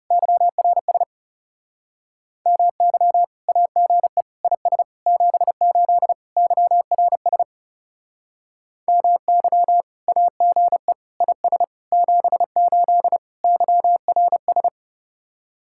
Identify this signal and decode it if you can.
Morse code, heard as a keyed tone, about 37 wpm, 700 Hz. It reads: YRS MY AGE IS 78 YRS MY AGE IS 78 YRS
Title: Speed-Racing Common QSO Phrases - 30wpm